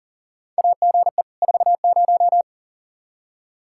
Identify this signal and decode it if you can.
Morse code, heard as a keyed tone, about 40 wpm, 700 Hz. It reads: AGE 40
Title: Practice QSO - Exchange 3 - More Info - 40wpm